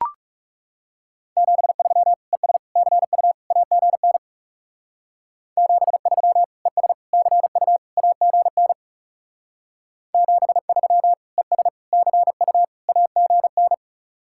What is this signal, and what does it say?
Morse code, heard as a keyed tone, about 40 wpm, 700 Hz. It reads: 73 ES CU AGN 73 ES CU AGN 73 ES CU AGN